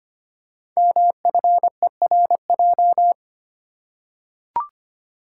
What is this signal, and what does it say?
Morse code, heard as a keyed tone, about 25 wpm, 700 Hz. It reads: MFERJ
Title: Code Groups - 5 Letters - 25wpm